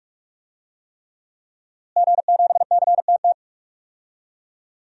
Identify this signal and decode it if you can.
Morse code, heard as a keyed tone, about 42 wpm, 700 Hz. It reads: G7CTT
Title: DXCC-100 Call Signs - 5 Characters - Repeated 3x - 45wpm